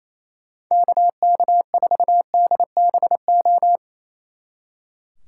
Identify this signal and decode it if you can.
Morse code, heard as a keyed tone, about 34 wpm, 700 Hz. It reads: KK4DBO
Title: Speed Racing - USA 2x3 Call Signs 28wpm